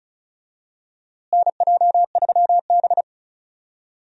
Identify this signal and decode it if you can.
Morse code, heard as a keyed tone, about 34 wpm, 700 Hz. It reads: NJ3B
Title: USA Call Signs - 4 Characters - Repeated 2x - 35wpm